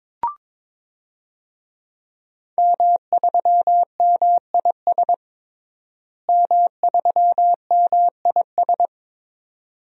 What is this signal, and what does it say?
Morse code, heard as a keyed tone, about 22 wpm, 700 Hz. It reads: M3MIS M3MIS
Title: DXCC-100 Call Signs - 5 Characters - Repeated 2x - 22wpm